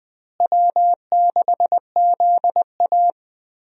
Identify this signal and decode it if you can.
Morse code, heard as a keyed tone, about 20 wpm, 700 Hz. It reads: W6ZA
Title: USA Call Signs - 4 Characters - Repeated 2x - 20wpm